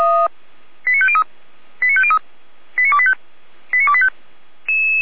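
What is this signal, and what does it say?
Five-tone selective call (ZVEI), heard as quick runs of five tones, tones of about 70 ms. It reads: 89503 89503 89285 89285
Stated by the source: A ZVEI five-tone selective call received by SDR, logged on the Feuerwehr Orlamünde audio server.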